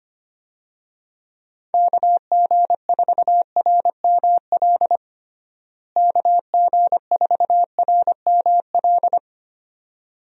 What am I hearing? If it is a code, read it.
Morse code, heard as a keyed tone, about 25 wpm, 700 Hz. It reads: KG4RML KG4RML
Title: USA Call Signs - 6 Characters - Repeated 2x - 25wpm